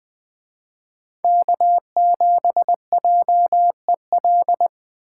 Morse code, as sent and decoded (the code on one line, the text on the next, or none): -.- --... .--- . .-..
K7JEL